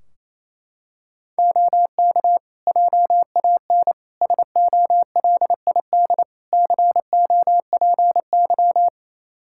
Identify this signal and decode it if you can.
Morse code, heard as a keyed tone, about 28 wpm, 700 Hz. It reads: OK JAN SOLID COPY
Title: K1USN SST QSO Practice 28wpm